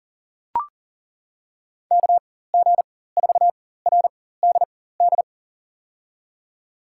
Morse code, heard as a keyed tone, about 37 wpm, spaced as for 18 wpm, 700 Hz. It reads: KG4RDD
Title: USA Call Signs - 6 Characters - 4x Character Spacing - 40wpm